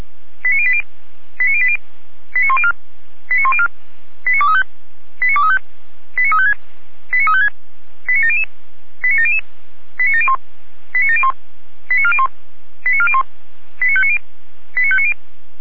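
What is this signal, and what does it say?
Five-tone selective call (ZVEI), heard as quick runs of five tones, tones of about 70 ms. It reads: 89080 89080 89104 89104 89235 89235 89356 89356 89700 89700 89701 89701 89401 89401 89590 89590
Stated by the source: ZVEI five-tone selective call received on an SDR.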